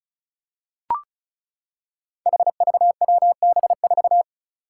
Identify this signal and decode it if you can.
Morse code, heard as a keyed tone, about 36 wpm, 700 Hz. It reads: HVWB4
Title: Five Challenging Letter-Number Combinations 35wpm